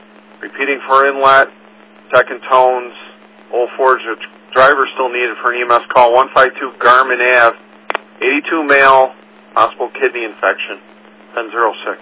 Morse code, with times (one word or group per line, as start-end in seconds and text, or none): none